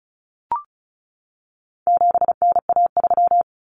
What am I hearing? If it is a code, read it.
Morse code, heard as a keyed tone, about 34 wpm, 700 Hz. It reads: 7NA3
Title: Common Cut Numbers - 4-Digit 35wpm